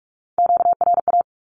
Morse code, heard as a keyed tone, about 44 wpm, 700 Hz. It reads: QRA
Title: Q-Codes 45wpm